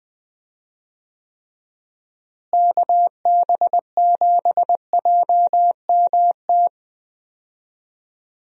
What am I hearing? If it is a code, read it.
Morse code, heard as a keyed tone, about 20 wpm, 700 Hz. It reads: KB7JMT
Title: USA Call Signs - 6 Characters - Repeated 3x - 20wpm